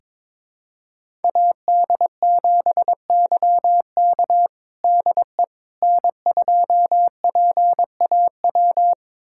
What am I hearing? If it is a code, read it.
Morse code, heard as a keyed tone, about 22 wpm, 700 Hz. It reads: AD7YK DE N2PAW
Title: USA Callsign DE Callsign - 5 Characters - 22wpm